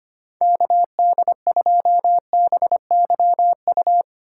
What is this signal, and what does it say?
Morse code, heard as a keyed tone, about 25 wpm, 690 Hz. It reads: KD2BYU
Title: USA Call Signs - 6 Characters - 3x Character Spacing - 25wpm